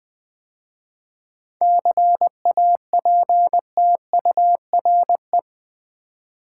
Morse code, heard as a keyed tone, about 20 wpm, 700 Hz. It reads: CAPTURE
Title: 7 Letter Words - First 3 Letters Given - 20wpm